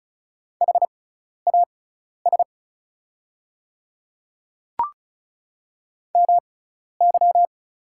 Morse code, heard as a keyed tone, about 36 wpm, spaced as for 12 wpm, 690 Hz. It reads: HAS MY
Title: 6x Character Spacing - Top 300 Words - 35wpm